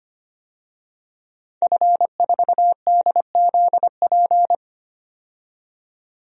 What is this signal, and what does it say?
Morse code, heard as a keyed tone, about 25 wpm, 700 Hz. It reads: F4DZP
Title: DXCC-100 Call Signs - 5 Characters - 25wpm